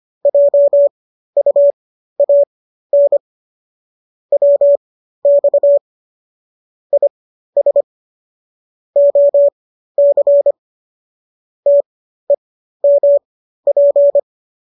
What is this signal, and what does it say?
Morse code, heard as a keyed tone, about 25 wpm, spaced as for 13 wpm, 570 Hz. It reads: JUAN WX IS OC TEMP